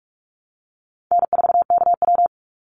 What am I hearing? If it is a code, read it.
Morse code, heard as a keyed tone, about 44 wpm, 700 Hz. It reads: N4KW
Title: Easy Call Signs Repeated 2x - 45wpm